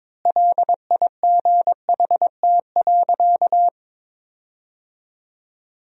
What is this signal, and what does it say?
Morse code, heard as a keyed tone, about 22 wpm, 700 Hz. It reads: LIGHT.
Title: 4x Word-Spacing - Sentences from Top 200 Words - 22wpm